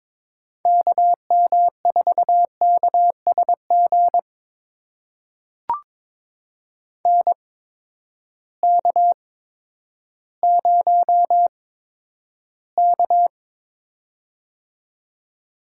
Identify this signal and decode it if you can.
Morse code, heard as a keyed tone, about 22 wpm, spaced as for 11 wpm, 700 Hz. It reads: KM4KSG NK0K
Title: USA Call Signs - 8x Character Spacing - 22wpm